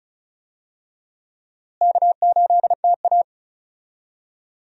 Morse code, heard as a keyed tone, about 36 wpm, 700 Hz. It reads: K8TA
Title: USA Call Signs - 4 Characters - Repeated 2x - 35wpm